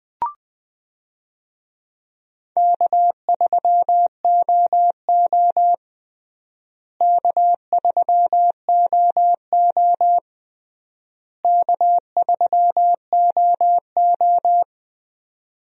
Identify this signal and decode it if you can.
Morse code, heard as a keyed tone, about 20 wpm, 700 Hz. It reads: K3OO K3OO K3OO
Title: USA Call Signs - 4 Characters - Repeated 3x - 20wpm